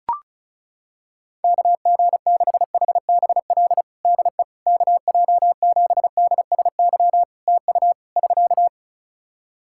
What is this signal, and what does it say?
Morse code, heard as a keyed tone, about 34 wpm, 700 Hz. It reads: KG6HBL DE KJ7DSY TU <SK>
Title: K1USN SST QSO Practice 35wpm